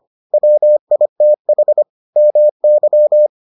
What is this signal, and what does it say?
Morse code, heard as a keyed tone, about 25 wpm, 600 Hz. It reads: WITH MY